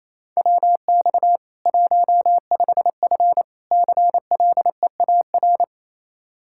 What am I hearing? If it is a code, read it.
Morse code, heard as a keyed tone, about 28 wpm, 700 Hz. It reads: WX 15F CLEAR
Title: Practice QSO - Exchange 2 - Info - 28wpm